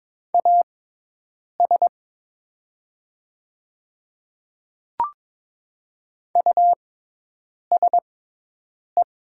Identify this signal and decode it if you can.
Morse code, heard as a keyed tone, about 23 wpm, spaced as for 8 wpm, 700 Hz. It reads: AS USE
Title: 6x Character Spacing - Top 200 Words - 22wpm